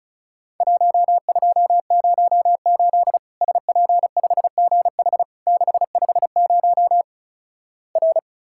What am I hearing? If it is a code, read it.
Morse code, heard as a keyed tone, about 36 wpm, 700 Hz. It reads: 1208 SP5GH 650 R
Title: Worked All Europe DX Contest - QTC Exchange - 35wpm